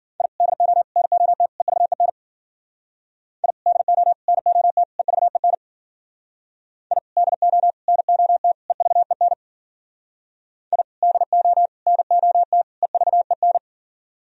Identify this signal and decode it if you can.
Morse code, heard as a keyed tone, about 52 wpm, 700 Hz. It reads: I DO NOT EVEN I DO NOT EVEN I DO NOT EVEN I DO NOT EVEN